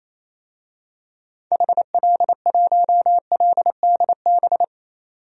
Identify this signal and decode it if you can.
Morse code, heard as a keyed tone, about 28 wpm, 700 Hz. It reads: HL1LDB
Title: DXCC-100 Call Signs - 6 Characters - 28wpm